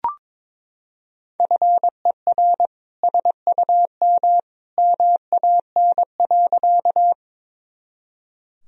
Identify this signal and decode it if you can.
Morse code, heard as a keyed tone, about 22 wpm, 700 Hz. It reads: FER SUM MAN.
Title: Sentences from Top 200 Words and Abbreviations 22wpm